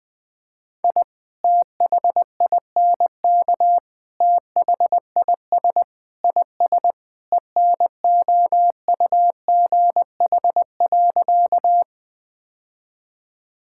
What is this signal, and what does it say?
Morse code, heard as a keyed tone, about 20 wpm, 690 Hz. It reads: I THINK THIS IS ENOUGH.
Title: Sentences with 4-Word Phrases - Spoken Beginning - 20wpm